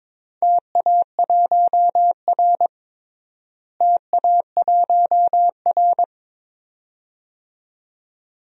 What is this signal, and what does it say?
Morse code, heard as a keyed tone, about 22 wpm, 700 Hz. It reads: TA1R TA1R
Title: DXCC-100 Call Signs - 4 Characters - Repeated 2x - 22wpm